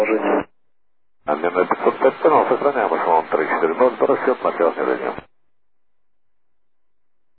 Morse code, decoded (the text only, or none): none